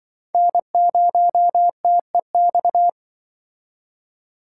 Morse code, heard as a keyed tone, about 26 wpm, 700 Hz. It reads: N0TEX